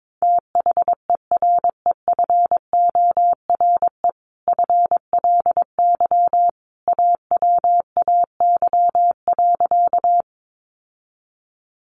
Morse code, heard as a keyed tone, about 22 wpm, 700 Hz. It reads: THEREFORE FLY AWAY.
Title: Sentences from Top 800 Words 22wpm